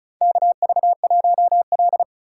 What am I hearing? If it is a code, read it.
Morse code, heard as a keyed tone, about 36 wpm, 700 Hz. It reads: KV1L
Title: USA Call Signs - 4 Characters - 3x Character Spacing - 35wpm